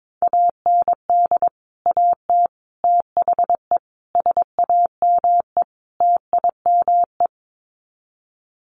Morse code, heard as a keyed tone, about 22 wpm, 700 Hz. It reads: AND AT THE SAME TIME
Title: Sets of 5 Words 22wpm